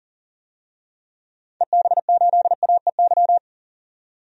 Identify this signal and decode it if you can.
Morse code, heard as a keyed tone, about 40 wpm, 700 Hz. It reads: EB8AEY
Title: DXCC-100 Call Signs - 6 Characters - 40wpm